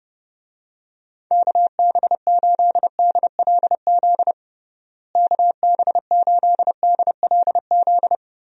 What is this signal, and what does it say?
Morse code, heard as a keyed tone, about 32 wpm, 700 Hz. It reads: KB8DLZ KB8DLZ
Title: USA Call Signs - 6 Characters - Repeated 3x - 30wpm